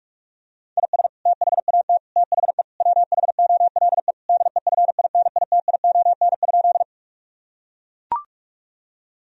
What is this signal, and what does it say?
Morse code, heard as a keyed tone, about 42 wpm, 700 Hz. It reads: IS THAT THE WHOLE DEFINITION?
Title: Sentences from Top 1000 Words - Review Part 4 - 45wpm